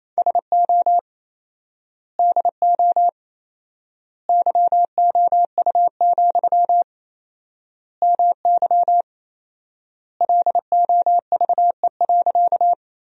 Morse code, heard as a keyed tone, about 28 wpm, 700 Hz. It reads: SO DO YOU, MY LOVE.
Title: 4x Word-Spacing - Sentences from Top 400 Words - 28wpm